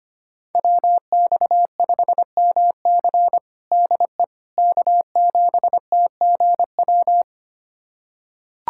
Morse code, heard as a keyed tone, about 25 wpm, 700 Hz. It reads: WX5MC DE K7TGW
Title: USA Callsign DE Callsign - 5 Characters - 25wpm